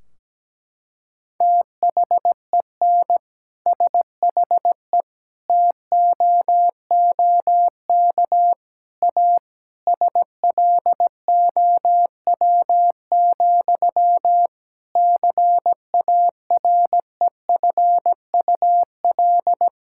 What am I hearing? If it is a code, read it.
Morse code, heard as a keyed tone, about 17 wpm, 700 Hz. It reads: THEN SHE TOOK A SLOW, CAREFUL
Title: Sentences from Top 1600 Words 17wpm